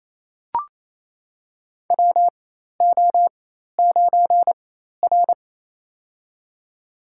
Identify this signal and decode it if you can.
Morse code, heard as a keyed tone, about 27 wpm, spaced as for 13 wpm, 700 Hz. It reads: WO9R
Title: USA Call Signs - 4 Characters - 4x Character Spacing - 28wpm